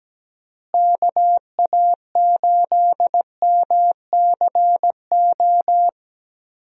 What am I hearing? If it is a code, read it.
Morse code, heard as a keyed tone, about 17 wpm, 700 Hz. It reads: KA8MCO
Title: USA Call Signs - 6 Characters - Repeated 3x - 17wpm